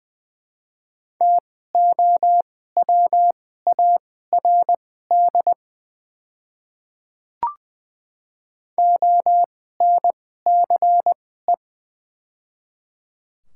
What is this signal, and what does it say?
Morse code, heard as a keyed tone, about 21 wpm, spaced as for 15 wpm, 700 Hz. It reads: TOWARD ONCE
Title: 2x Character Spacing - Top 500 Words - 20wpm